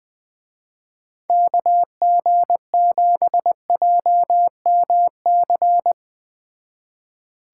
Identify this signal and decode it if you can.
Morse code, heard as a keyed tone, about 20 wpm, 700 Hz. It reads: KG7JMC
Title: Speed Racing - USA 2x3 Call Signs 17wpm